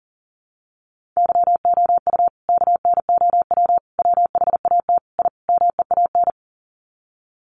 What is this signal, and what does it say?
Morse code, heard as a keyed tone, about 40 wpm, 700 Hz. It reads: YOU KNOW WHAT I MEAN
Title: Sets of 5 Words 40wpm